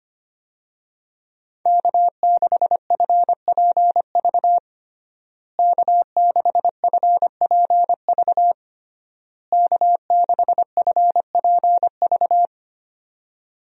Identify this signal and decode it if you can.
Morse code, heard as a keyed tone, about 25 wpm, 700 Hz. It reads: K6FPV K6FPV K6FPV
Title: USA Call Signs - 5 Characters - Repeated 3x - 25wpm